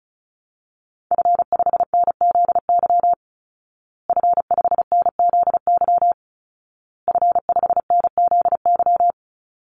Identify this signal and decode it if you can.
Morse code, heard as a keyed tone, about 34 wpm, 700 Hz. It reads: F5NZY F5NZY F5NZY
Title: Call Signs Repeated 2x - 35wpm